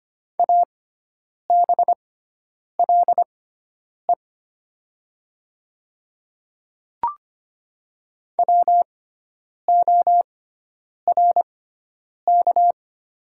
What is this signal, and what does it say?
Morse code, heard as a keyed tone, about 25 wpm, spaced as for 9 wpm, 700 Hz. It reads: ABLE WORK
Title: 6x Character Spacing - Top 500 Words - 25wpm